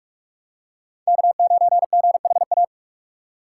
Morse code, heard as a keyed tone, about 46 wpm, 700 Hz. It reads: K9GHA
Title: USA Call Signs - 5 Characters - 4x Character Spacing - 45wpm